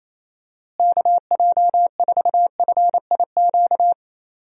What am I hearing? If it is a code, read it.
Morse code, heard as a keyed tone, about 28 wpm, 700 Hz. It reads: KJ4FIQ
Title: USA Call Signs - 6 Characters - Repeated 3x - 28wpm